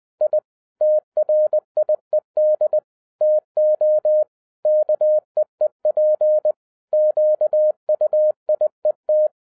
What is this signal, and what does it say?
Morse code, heard as a keyed tone, about 20 wpm, 610 Hz. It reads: I TRIED TO KEEP QUIET
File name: 2011-07-mp3 / n58 BACK FROM A BIG ACCIDENT 20wpm